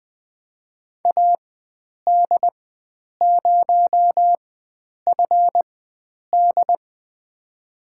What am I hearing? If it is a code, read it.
Morse code, heard as a keyed tone, about 20 wpm, spaced as for 9 wpm, 700 Hz. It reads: AD0FD
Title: USA Call Signs - 5 Characters - 4x Character Spacing - 20wpm